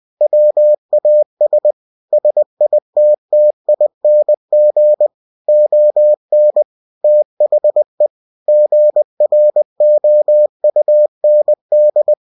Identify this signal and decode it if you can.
Morse code, heard as a keyed tone, about 20 wpm, 600 Hz. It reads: WAS SITTING ON THE GROUND
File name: cw-books / winnie-the-pooh / winnie-20wpm-page-0078